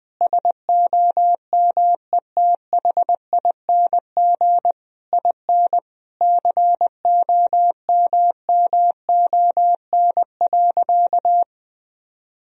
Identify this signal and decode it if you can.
Morse code, heard as a keyed tone, about 20 wpm, 700 Hz. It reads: SOMETHING IN COMMON.
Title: Sentences from Top 1000 Words - Review Part 3 - 20wpm